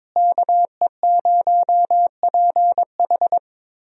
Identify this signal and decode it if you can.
Morse code, heard as a keyed tone, about 22 wpm, 700 Hz. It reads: KE0PH
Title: USA Call Signs - 5 Characters - Repeated 3x - 22wpm